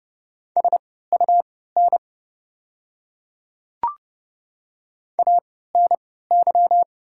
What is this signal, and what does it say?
Morse code, heard as a keyed tone, about 30 wpm, spaced as for 17 wpm, 700 Hz. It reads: SUN ANY
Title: 3x Character Spacing - Top 500 Words - 30wpm